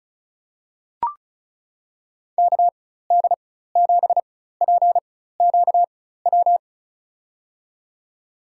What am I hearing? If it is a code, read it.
Morse code, heard as a keyed tone, about 36 wpm, spaced as for 16 wpm, 700 Hz. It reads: KD7PQW
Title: USA Call Signs - 6 Characters - 4x Character Spacing - 35wpm